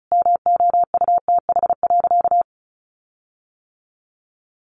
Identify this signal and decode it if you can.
Morse code, heard as a keyed tone, about 34 wpm, 700 Hz. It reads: MOUTH.
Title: Sentences from Top 1500 Words 35wpm